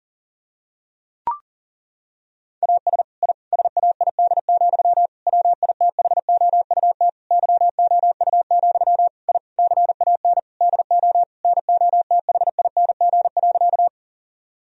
Morse code, heard as a keyed tone, about 37 wpm, 700 Hz. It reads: AS I SAID, WITHOUT YOU, I CAN DO NOTHING.
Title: Sentences from Top 1000 Words - Review Part 3 - 40wpm